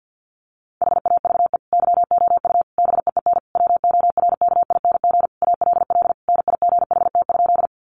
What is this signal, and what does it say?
Morse code, heard as a keyed tone, about 48 wpm, 700 Hz. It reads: HAVE YOU BEEN WORKING ALL NIGHT?
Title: Sentences from Top 700 Words 50wpm